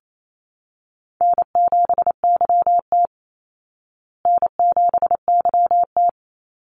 Morse code, heard as a keyed tone, about 28 wpm, 700 Hz. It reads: N7YT N7YT
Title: Easy Call Signs Repeated 2x - 28wpm